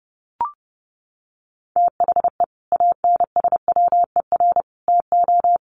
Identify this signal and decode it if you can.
Morse code, heard as a keyed tone, about 28 wpm, 700 Hz. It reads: THE ANSWER TO
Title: Sets of 5 Words Encore2 30wpm